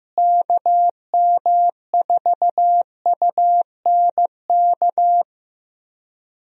Morse code, heard as a keyed tone, about 16 wpm, 700 Hz. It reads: KM4UNK
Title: USA Call Signs - 6 Characters - Repeated 2x - 15wpm